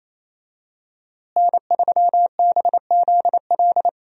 Morse code, heard as a keyed tone, about 28 wpm, 700 Hz. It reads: N3BZL
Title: USA Call Signs - 5 Characters - Repeated 3x - 28wpm